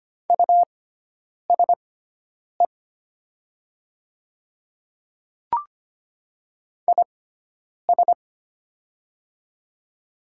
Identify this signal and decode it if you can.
Morse code, heard as a keyed tone, about 25 wpm, spaced as for 9 wpm, 700 Hz. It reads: USE IS
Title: 6x Character Spacing - Top 500 Words - 25wpm